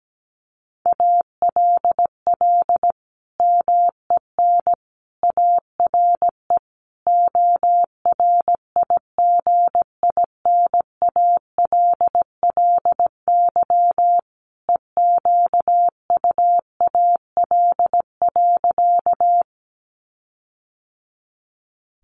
Morse code, heard as a keyed tone, about 17 wpm, 700 Hz. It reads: ALL MEN ARE ORIGINALLY EQUAL.
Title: Sentences from Top 1500 Words 17wpm